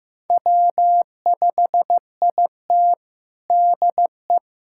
Morse code, heard as a keyed tone, about 15 wpm, 700 Hz. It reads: W5IT DE
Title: USA Callsign DE Callsign - 4 Characters - 15wpm